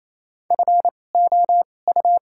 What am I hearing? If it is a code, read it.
Morse code, heard as a keyed tone, about 28 wpm, spaced as for 20 wpm, 700 Hz. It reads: FOU
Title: 2x Character Spacing - Top 500 Words - 28wpm